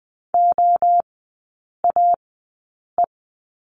Morse code, heard as a keyed tone, about 20 wpm, spaced as for 8 wpm, 700 Hz. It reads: OAE
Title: 041 Lesson 8e - Number 1 - Three Characters 20wpm